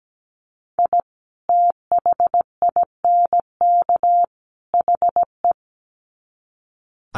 Morse code, heard as a keyed tone, about 17 wpm, 700 Hz. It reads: I THINK HE
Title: Sets of 3 Words 17wpm